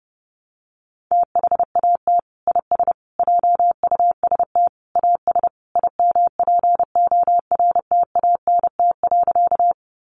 Morse code, heard as a keyed tone, about 30 wpm, 700 Hz. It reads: THAT IS JUST AS IMPORTANT.
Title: Sentences from Top 600 Words 30wpm